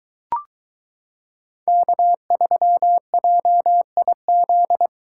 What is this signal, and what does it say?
Morse code, heard as a keyed tone, about 22 wpm, 700 Hz. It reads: K3JIZ